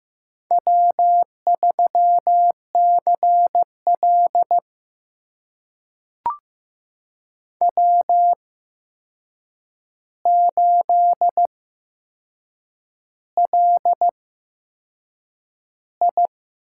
Morse code, heard as a keyed tone, about 15 wpm, spaced as for 6 wpm, 700 Hz. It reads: W3CL W8LI
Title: USA Call Signs - 4 Characters - 8x Character Spacing - 15wpm